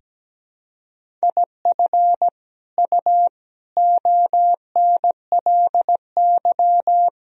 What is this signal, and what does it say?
Morse code, heard as a keyed tone, about 17 wpm, 700 Hz. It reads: IF U ONLY
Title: Sentences from Top 400 Words and Abbreviations 17wpm